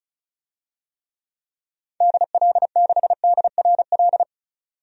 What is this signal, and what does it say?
Morse code, heard as a keyed tone, about 36 wpm, 700 Hz. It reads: DL6DRL
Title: DXCC-100 Call Signs - 6 Characters - 35wpm